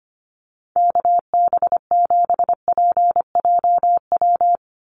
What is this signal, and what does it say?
Morse code, heard as a keyed tone, about 25 wpm, 700 Hz. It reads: KB7PJW
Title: USA Call Signs 25wpm